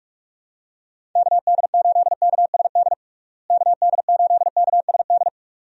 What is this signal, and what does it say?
Morse code, heard as a keyed tone, about 46 wpm, 700 Hz. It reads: KD8KSD KD8KSD
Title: USA Call Signs - 6 Characters - Repeated 2x - 45wpm